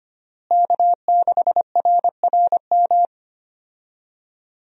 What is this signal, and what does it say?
Morse code, heard as a keyed tone, about 25 wpm, 700 Hz. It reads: K6RRM
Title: USA Call Signs - 5 Characters - Repeated 2x - 25wpm